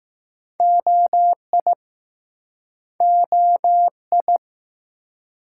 Morse code, heard as a keyed tone, about 20 wpm, 700 Hz. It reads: OI OI